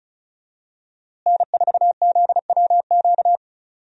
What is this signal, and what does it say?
Morse code, heard as a keyed tone, about 36 wpm, 700 Hz. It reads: N4ZWQ
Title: USA Call Signs - 5 Characters - Repeated 3x - 35wpm